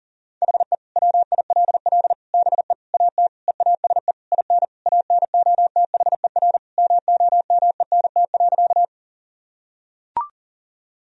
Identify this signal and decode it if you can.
Morse code, heard as a keyed tone, about 40 wpm, 700 Hz. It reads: HE WILL BE AT EASE IN ANOTHER MOMENT.
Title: Sentences from Top 1000 Words - Review Part 1 - 40wpm